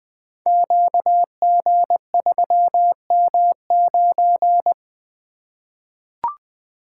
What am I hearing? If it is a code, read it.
Morse code, heard as a keyed tone, about 20 wpm, 700 Hz. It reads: QG3M9
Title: Code Groups - 5 Letter:Numbers - 20wpm